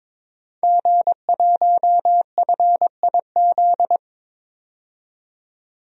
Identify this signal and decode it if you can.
Morse code, heard as a keyed tone, about 22 wpm, 700 Hz. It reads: G1FIZ
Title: DXCC-100 Call Signs - 5 Characters - 22wpm